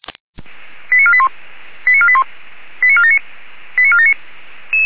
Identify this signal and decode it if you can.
Five-tone selective call (ZVEI), heard as quick runs of five tones, tones of about 70 ms. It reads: 89481 89481 89479 89479
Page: ZVEI